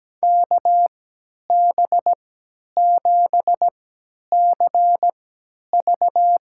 Morse code, heard as a keyed tone, about 17 wpm, spaced as for 10 wpm, 700 Hz. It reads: KB7CV
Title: USA Call Signs - 3x Character Spacing - 17wpm